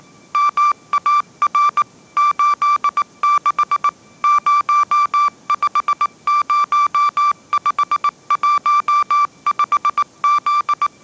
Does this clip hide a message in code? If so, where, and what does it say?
Yes: Morse code from 0.3 to 10.9 s, heard as a keyed tone, about 18 wpm, 1170 Hz. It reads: MAR 86 050515Z